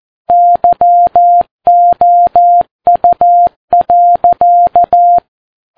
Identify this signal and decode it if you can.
Morse code, heard as a keyed tone, about 14 wpm, 700 Hz. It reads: YOU.
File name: Ham Radio Sat